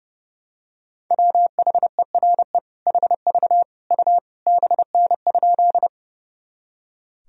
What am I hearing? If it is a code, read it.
Morse code, heard as a keyed tone, about 30 wpm, 700 Hz. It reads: WHERE HV U BN?
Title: Sentences from Top 200 Words and Abbreviations 30wpm